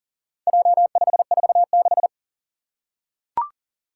Morse code, heard as a keyed tone, about 40 wpm, 700 Hz. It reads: J546
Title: Four Challenging Letter-Number Combinations 40wpm